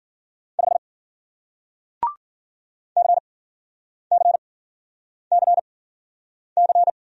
Morse code, heard as a keyed tone, about 53 wpm, spaced as for 9 wpm, 700 Hz. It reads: 5 CCCC